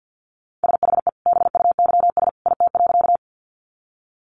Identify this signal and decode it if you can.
Morse code, heard as a keyed tone, about 50 wpm, 700 Hz. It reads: SHE BUYS IT.
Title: Additional Easy Phrases 50wpm